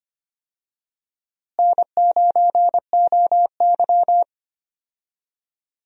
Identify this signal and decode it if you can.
Morse code, heard as a keyed tone, about 25 wpm, 700 Hz. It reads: N9OY
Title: USA Call Signs - 4 Characters - Repeated 2x - 25wpm